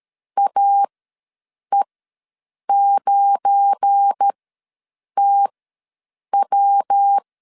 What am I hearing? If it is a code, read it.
Morse code, heard as a keyed tone, about 13 wpm, spaced as for 7 wpm, 800 Hz. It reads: AE9TW
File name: LEZIONI / 7 GF9Z / 7 GF9 ws 12wpm 3344